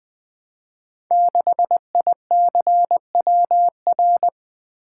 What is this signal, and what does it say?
Morse code, heard as a keyed tone, about 20 wpm, 700 Hz. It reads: 6ICWR
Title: Code Groups - 5 Letter:Numbers - 20wpm